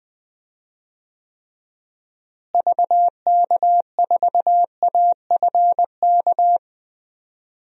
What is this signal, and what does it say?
Morse code, heard as a keyed tone, about 20 wpm, 700 Hz. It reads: VK4AFK
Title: DXCC-100 Call Signs - 6 Characters - Repeated 3x - 20wpm